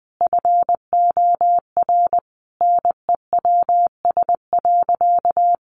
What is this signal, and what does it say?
Morse code, heard as a keyed tone, about 20 wpm, 700 Hz. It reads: FOR NEWS.
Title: Sentences from Top 800 Words 20wpm